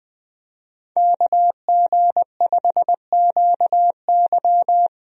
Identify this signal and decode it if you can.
Morse code, heard as a keyed tone, about 21 wpm, 700 Hz. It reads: KG5QY
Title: USA Call Signs - 5 Characters - 20wpm